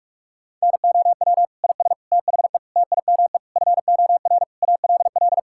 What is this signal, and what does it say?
Morse code, heard as a keyed tone, about 46 wpm, 700 Hz. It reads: NOW IS THE TIME FOR ALL
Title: Pangram Challenge v2 - 45wpm